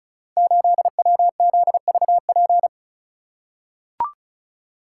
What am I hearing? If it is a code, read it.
Morse code, heard as a keyed tone, about 36 wpm, 700 Hz. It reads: 8WZVP
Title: Code Groups - 5 Letter:Numbers - 35wpm